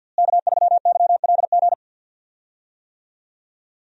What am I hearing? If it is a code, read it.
Morse code, heard as a keyed tone, about 52 wpm, 700 Hz. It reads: K3YLG
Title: USA Call Signs - 5 Characters - Repeated 2x - 50wpm